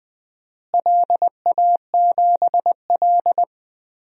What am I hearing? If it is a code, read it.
Morse code, heard as a keyed tone, about 20 wpm, 700 Hz. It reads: LA7L
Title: DXCC-100 Call Signs - 4 Characters - Repeated 2x - 20wpm